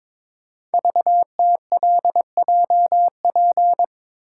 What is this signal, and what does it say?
Morse code, heard as a keyed tone, about 22 wpm, 700 Hz. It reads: VTLJP
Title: Code Groups - 5 Letters - 22wpm